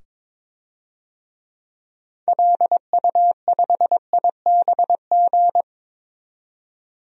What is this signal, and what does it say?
Morse code, heard as a keyed tone, about 22 wpm, 700 Hz. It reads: LU5IBG
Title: DXCC-100 Call Signs - 6 Characters - 22wpm